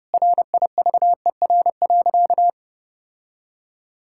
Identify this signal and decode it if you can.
Morse code, heard as a keyed tone, about 30 wpm, 700 Hz. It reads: RIVER.
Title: Sentences from Top 1000 Words - Review Part 5 - 30wpm